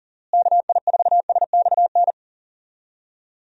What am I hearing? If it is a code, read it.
Morse code, heard as a keyed tone, about 40 wpm, 700 Hz. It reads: KI4SXN
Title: USA Call Signs - 6 Characters - Repeated 3x - 40wpm